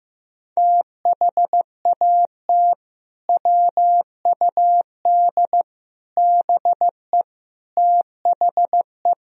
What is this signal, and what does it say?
Morse code, heard as a keyed tone, about 16 wpm, 700 Hz. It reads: THAT WUD BE THE
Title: Sentences from Top 400 Words and Abbreviations 15wpm